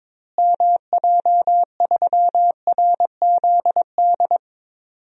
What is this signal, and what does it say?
Morse code, heard as a keyed tone, about 22 wpm, 700 Hz. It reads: MJ3RZD
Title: DXCC-100 Call Signs - 6 Characters - Repeated 3x - 22wpm